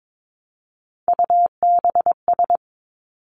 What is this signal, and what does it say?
Morse code, heard as a keyed tone, about 22 wpm, 700 Hz. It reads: UBS
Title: Three Challenging Letter-Number Combinations 22wpm